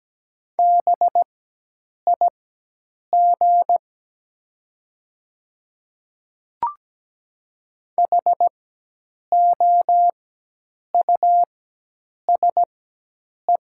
Morse code, heard as a keyed tone, about 17 wpm, spaced as for 8 wpm, 700 Hz. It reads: BIG HOUSE
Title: 4x Character Spacing - Top 300 Words - 17wpm